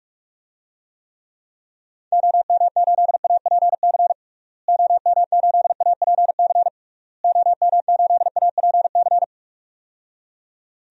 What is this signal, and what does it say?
Morse code, heard as a keyed tone, about 41 wpm, 700 Hz. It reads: OM8APC OM8APC OM8APC
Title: DXCC-100 Call Signs - 6 Characters - Repeated 3x - 45wpm